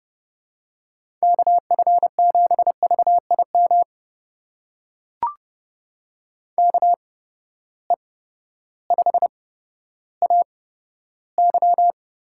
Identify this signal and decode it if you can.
Morse code, heard as a keyed tone, about 30 wpm, spaced as for 14 wpm, 700 Hz. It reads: KF7VIM KE5AY
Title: USA Call Signs - 6 Characters - 8x Character Spacing - 30wpm